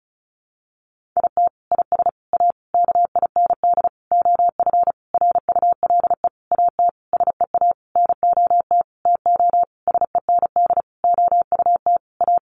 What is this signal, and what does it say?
Morse code, heard as a keyed tone, about 34 wpm, 700 Hz. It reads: IT IS A KIND OF RULE AT SEA NOT TO SEND OUT A
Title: Sentences from Top 1500 Words 35wpm